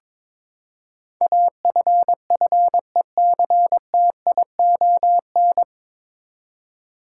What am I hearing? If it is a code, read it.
Morse code, heard as a keyed tone, about 22 wpm, 700 Hz. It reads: AFFECTION
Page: Morse Code Ninja practice 22wpm